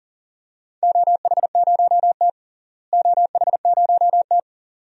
Morse code, heard as a keyed tone, about 38 wpm, 700 Hz. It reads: OH0T OH0T
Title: DXCC-100 Call Signs - 4 Characters - Repeated 2x - 40wpm